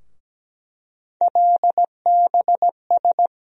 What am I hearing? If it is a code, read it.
Morse code, heard as a keyed tone, about 17 wpm, 700 Hz. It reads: LBS
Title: USA Call Sign Postfixes - 3 Letters 17wpm